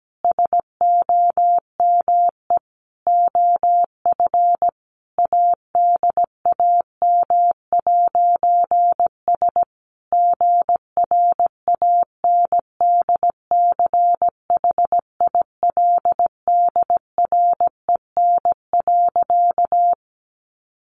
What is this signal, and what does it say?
Morse code, heard as a keyed tone, about 17 wpm, 700 Hz. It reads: SOME OF ADAM'S GRANDCHILDREN.
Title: A Sound Of Thunder 17wpm